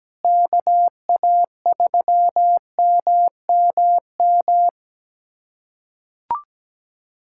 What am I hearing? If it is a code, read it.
Morse code, heard as a keyed tone, about 17 wpm, 700 Hz. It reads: KA3MMM
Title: USA Callsign DE Callsign 17wpm